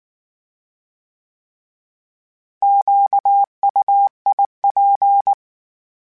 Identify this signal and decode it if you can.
Morse code, heard as a keyed tone, about 19 wpm, 800 Hz. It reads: QUIP